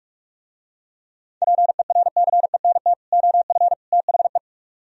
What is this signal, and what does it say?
Morse code, heard as a keyed tone, about 44 wpm, 700 Hz. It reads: PERCENT OF THE